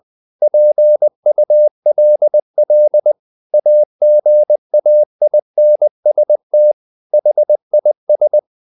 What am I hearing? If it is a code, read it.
Morse code, heard as a keyed tone, about 20 wpm, 600 Hz. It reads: PULL AGAINST HIS
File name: the old man and the sea / the old man and the sea part 1 20wpm